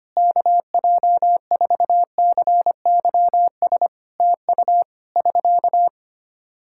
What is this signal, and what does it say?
Morse code, heard as a keyed tone, about 25 wpm, 700 Hz. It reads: KJ4CYS TU <SK>
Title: Practice QSO - Exchange 4 - Ending - 25wpm